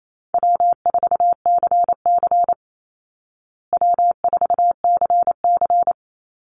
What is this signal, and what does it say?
Morse code, heard as a keyed tone, about 28 wpm, 700 Hz. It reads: W4CC W4CC
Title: Easy Call Signs Repeated 3x - 28wpm